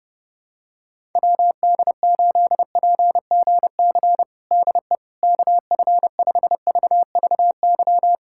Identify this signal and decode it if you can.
Morse code, heard as a keyed tone, about 32 wpm, 700 Hz. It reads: WD8PGC DE KF5VVY
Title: USA Callsign DE Callsign - 6 Characters - 30wpm